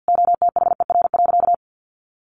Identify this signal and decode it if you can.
Morse code, heard as a keyed tone, about 50 wpm, 700 Hz. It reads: OTHER.
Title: Sentences from Top 1300 Words 50wpm